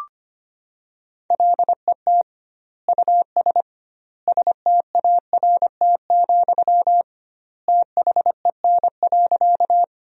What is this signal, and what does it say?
Morse code, heard as a keyed tone, about 25 wpm, 700 Hz. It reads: LET US START, THEN.
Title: 2x Word-Spacing - Sentences from Top 300 Words - 25wpm